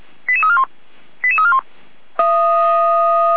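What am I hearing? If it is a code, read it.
Five-tone selective call (ZVEI), heard as quick runs of five tones, tones of about 70 ms. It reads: 88341 88341
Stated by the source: ZVEI five-tone selective call 88190 received by SDR.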